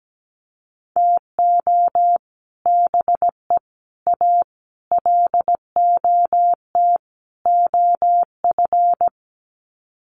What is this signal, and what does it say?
Morse code, heard as a keyed tone, about 17 wpm, 700 Hz. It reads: TO BE A LOT OF
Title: Sets of 5 Words Encore 17wpm